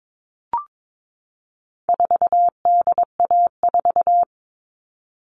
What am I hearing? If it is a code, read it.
Morse code, heard as a keyed tone, about 22 wpm, 700 Hz. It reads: 4DA4
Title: All Cut Numbers - 4-Digit 22wpm